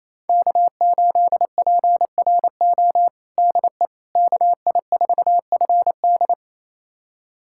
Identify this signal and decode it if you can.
Morse code, heard as a keyed tone, about 27 wpm, 700 Hz. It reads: K8PRO DE KI4FD
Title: USA Callsign DE Callsign - 5 Characters - 28wpm